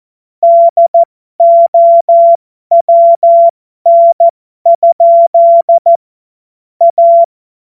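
Morse code, heard as a keyed tone, about 14 wpm, spaced as for 12 wpm, 680 Hz. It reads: DOWN? A
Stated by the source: Morse joke practice at effective 60, Farnsworth 70.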